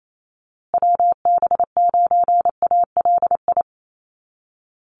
Morse code, heard as a keyed tone, about 28 wpm, 700 Hz. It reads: WB9ALI
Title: USA Call Signs Repeated 3x 28wpm